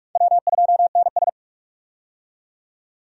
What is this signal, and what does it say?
Morse code, heard as a keyed tone, about 46 wpm, 700 Hz. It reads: W2NS
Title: USA Call Signs - 4 Characters - 45wpm